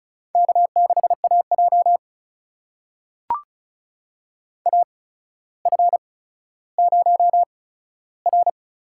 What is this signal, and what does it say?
Morse code, heard as a keyed tone, about 36 wpm, spaced as for 15 wpm, 700 Hz. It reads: K6AJ AF0R
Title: USA Call Signs - 4 Characters - 8x Character Spacing - 35wpm